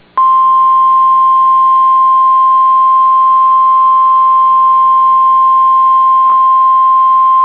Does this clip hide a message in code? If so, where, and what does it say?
Yes: Emergency Alert System (SAME) tones (the 1050 Hz attention tone) from 0.3 to 6.5 s.